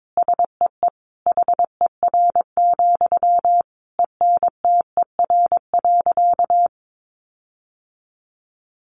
Morse code, heard as a keyed tone, about 22 wpm, 700 Hz. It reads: SEE HER, ENTER.
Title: Sentences from Top 1300 Words 22wpm